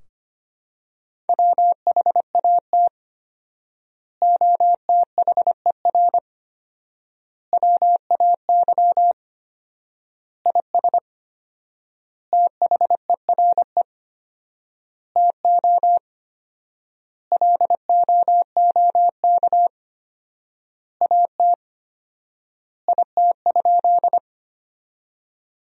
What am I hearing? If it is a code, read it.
Morse code, heard as a keyed tone, about 25 wpm, 700 Hz. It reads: WHAT OTHER WAY IS THERE TO LOOK AT IT?
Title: 4x Word-Spacing - Sentences from Top 100 Words - 25wpm